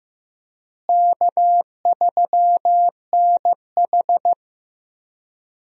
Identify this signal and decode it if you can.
Morse code, heard as a keyed tone, about 15 wpm, 700 Hz. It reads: K3NH
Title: USA Call Signs - 4 Characters - Repeated 3x - 15wpm